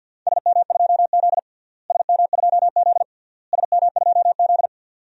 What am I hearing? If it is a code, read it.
Morse code, heard as a keyed tone, about 52 wpm, 700 Hz. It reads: SM2Z SM2Z SM2Z
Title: DXCC-100 Call Signs - 4 Characters - Repeated 3x - 50wpm